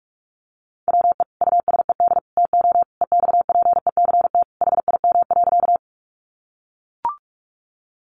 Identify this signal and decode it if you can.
Morse code, heard as a keyed tone, about 44 wpm, 700 Hz. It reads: WE USED TO EXPECT HIM.
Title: Sentences from Top 1100 Words 45wpm